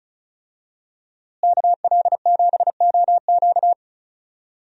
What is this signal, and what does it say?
Morse code, heard as a keyed tone, about 36 wpm, 700 Hz. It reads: KL7OQ
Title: DXCC-100 Call Signs - 5 Characters - Repeated 3x - 35wpm